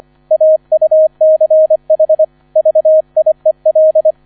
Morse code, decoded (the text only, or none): AUCH VIEL